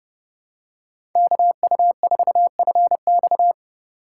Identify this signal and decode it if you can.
Morse code, heard as a keyed tone, about 31 wpm, 700 Hz. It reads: KU4FX
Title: USA Call Signs - 5 Characters - 2x Character Spacing - 30wpm